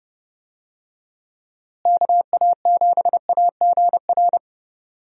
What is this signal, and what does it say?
Morse code, heard as a keyed tone, about 30 wpm, 700 Hz. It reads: KA7AGR
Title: USA Call Signs - 6 Characters - Repeated 2x - 30wpm